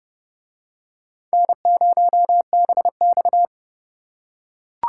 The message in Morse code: N0BX

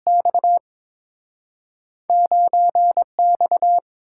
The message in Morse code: X 9X

